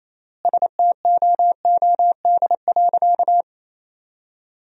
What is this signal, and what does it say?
Morse code, heard as a keyed tone, about 28 wpm, 700 Hz. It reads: STOOD.
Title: 3x Word-Spacing - Sentences from Top 500 Words - 28wpm